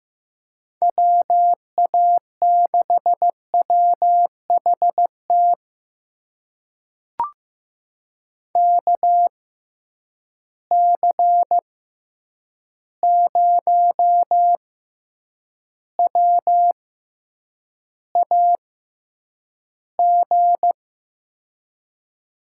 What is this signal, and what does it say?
Morse code, heard as a keyed tone, about 16 wpm, spaced as for 8 wpm, 700 Hz. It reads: WA6WHT KC0WAG
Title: USA Call Signs - 6 Characters - 6x Character Spacing - 15wpm